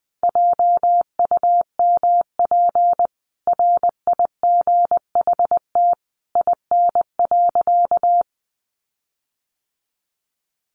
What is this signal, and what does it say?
Morse code, heard as a keyed tone, about 20 wpm, 700 Hz. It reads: JUMP RIGHT IN.
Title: Sentences from Top 1300 Words 20wpm